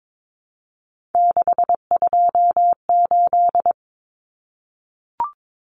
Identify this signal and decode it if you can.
Morse code, heard as a keyed tone, about 22 wpm, 700 Hz. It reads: 628
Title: Common Cut Numbers - 3-Digit 22wpm